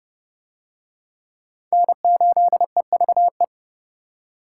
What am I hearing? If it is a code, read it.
Morse code, heard as a keyed tone, about 27 wpm, 700 Hz. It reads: N8EVE